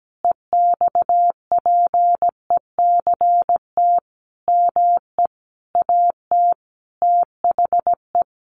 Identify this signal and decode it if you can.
Morse code, heard as a keyed tone, about 17 wpm, 700 Hz. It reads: EXPECT ME AT THE
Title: Sentences from Top 1100 Words 17wpm